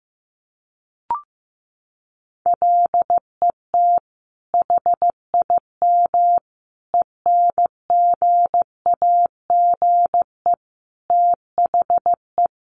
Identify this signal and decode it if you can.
Morse code, heard as a keyed tone, about 15 wpm, 700 Hz. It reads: LET HIM ENGAGE THE
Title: Sentences from Top 1500 Words 15wpm